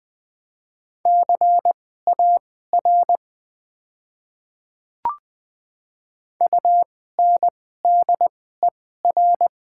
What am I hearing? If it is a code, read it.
Morse code, heard as a keyed tone, about 20 wpm, spaced as for 14 wpm, 700 Hz. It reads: CAR UNDER
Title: 2x Character Spacing - Top 300 Words - 20wpm